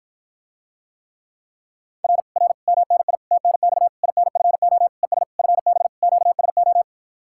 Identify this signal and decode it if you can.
Morse code, heard as a keyed tone, about 54 wpm, 700 Hz. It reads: R R MNI TNX INFO ES FB QSO